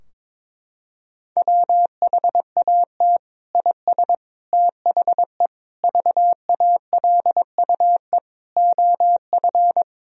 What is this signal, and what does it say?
Morse code, heard as a keyed tone, about 21 wpm, 700 Hz. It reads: WHAT IS THE VALUE OF
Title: Sentences from Top 1000 Words - Review Part 3 - 22wpm